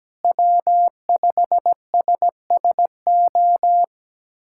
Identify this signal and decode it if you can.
Morse code, heard as a keyed tone, about 17 wpm, 700 Hz. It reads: W5SSO
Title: USA Callsign DE Callsign - 5 Characters - 17wpm